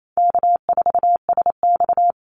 Morse code, heard as a keyed tone, about 28 wpm, 700 Hz. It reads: K4SX
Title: Easy Call Signs - 28wpm